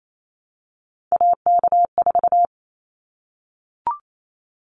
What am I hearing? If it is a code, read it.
Morse code, heard as a keyed tone, about 28 wpm, 700 Hz. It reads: AK4
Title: Callsign Prefixes for Top 25 DXCC - 28wpm